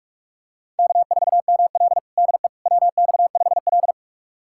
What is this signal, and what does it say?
Morse code, heard as a keyed tone, about 46 wpm, 700 Hz. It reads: K4ML DE WX5L
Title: USA Callsign DE Callsign - 4 Characters - 45wpm